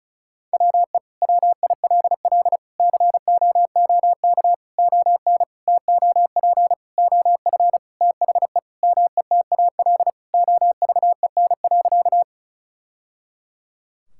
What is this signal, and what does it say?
Morse code, heard as a keyed tone, about 34 wpm, 700 Hz. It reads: WE WILL COOK ON TOP OF THE METAL OVEN.
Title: Sentences from Top 1000 Words - Review Part 6 - 35wpm